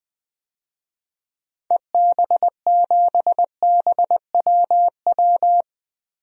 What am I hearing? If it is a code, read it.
Morse code, heard as a keyed tone, about 20 wpm, 700 Hz. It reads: EB7BWW
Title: DXCC-100 Call Signs - 6 Characters - Repeated 3x - 20wpm